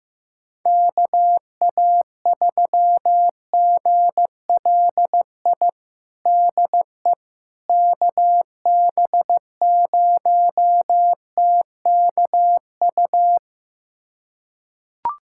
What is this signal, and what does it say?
Morse code, heard as a keyed tone, about 16 wpm, 700 Hz. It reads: KA3GLI DE KB0TKU
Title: USA Callsign DE Callsign - 6 Characters - 15wpm